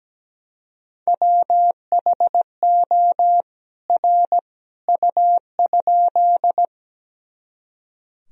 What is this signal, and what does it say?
Morse code, heard as a keyed tone, about 17 wpm, 700 Hz. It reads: WHO R U?
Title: Sentences from Top 100 Words and Abbreviations 17wpm